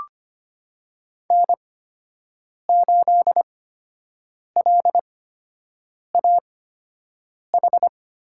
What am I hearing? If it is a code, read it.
Morse code, heard as a keyed tone, about 25 wpm, spaced as for 7 wpm, 700 Hz. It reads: N8LAH
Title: USA Call Signs - 5 Characters - 8x Character Spacing - 25wpm